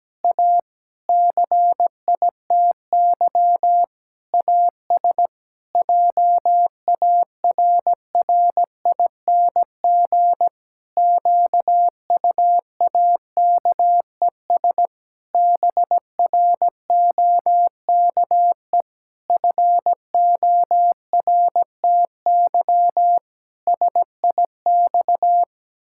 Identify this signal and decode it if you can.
Morse code, heard as a keyed tone, about 17 wpm, 700 Hz. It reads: A CITY AS JARRING QUAKES BROKE FORTY SIX
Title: Pangram Challenge v2 - 17wpm